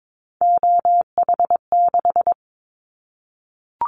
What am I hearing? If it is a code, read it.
Morse code, heard as a keyed tone, about 22 wpm, 700 Hz. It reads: OH6
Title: Callsign Prefixes for Top 50 DXCC - 22wpm